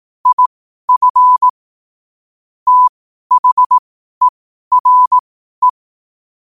Morse code, heard as a keyed tone, about 18 wpm, spaced as for 12 wpm, 1000 Hz. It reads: IF THERE
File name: qotdfiles / 12wpm